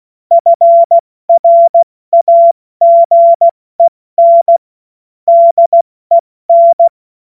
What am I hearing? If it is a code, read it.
Morse code, heard as a keyed tone, about 16 wpm, 680 Hz. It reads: FRAGEN DEN
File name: Morsekurs / Witze / Witz 21. Dezember 2021, 70 F80